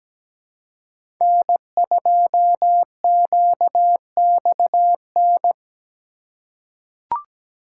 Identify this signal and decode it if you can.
Morse code, heard as a keyed tone, about 17 wpm, 700 Hz. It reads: N2QXN
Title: USA Call Signs - 5 Characters - 4x Character Spacing - 17wpm